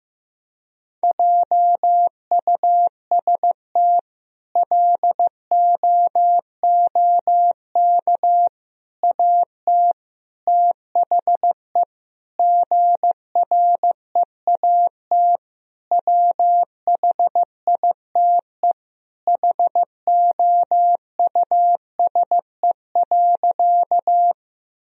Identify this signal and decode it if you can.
Morse code, heard as a keyed tone, about 15 wpm, 700 Hz. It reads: JUST LOOK AT THE GREAT WHITE HOUSE.
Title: Sentences from Top 1000 Words - Review Part 1 - 15wpm